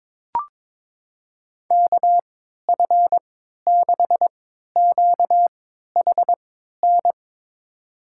Morse code, heard as a keyed tone, about 21 wpm, spaced as for 12 wpm, 700 Hz. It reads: KF6QHN
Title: USA Call Signs - 6 Characters - 3x Character Spacing - 22wpm